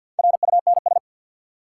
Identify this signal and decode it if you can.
Morse code, heard as a keyed tone, about 46 wpm, 700 Hz. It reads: RUNS
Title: Sentences from Top 600 Words - New Word Drill - 50wpm